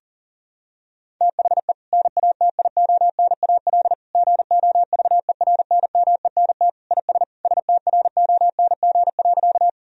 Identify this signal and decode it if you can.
Morse code, heard as a keyed tone, about 37 wpm, 700 Hz. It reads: THE NATIONAL GOVERNMENT IS STRONG.
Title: Sentences from Top 1000 Words - Review Part 4 - 40wpm